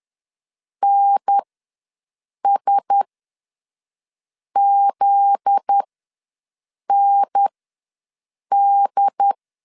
Morse code, heard as a keyed tone, about 10 wpm, spaced as for 5 wpm, 800 Hz. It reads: NSZND
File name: LEZIONI / 9 2 ws 10wpm 4130